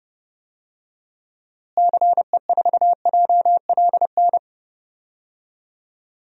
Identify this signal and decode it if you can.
Morse code, heard as a keyed tone, about 30 wpm, 700 Hz. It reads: CE4JLN
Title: DXCC-100 Call Signs - 6 Characters - Repeated 3x - 30wpm